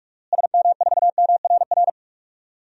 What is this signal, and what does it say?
Morse code, heard as a keyed tone, about 46 wpm, 700 Hz. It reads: SM4MRR
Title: DXCC-100 Call Signs - 6 Characters - Repeated 2x - 45wpm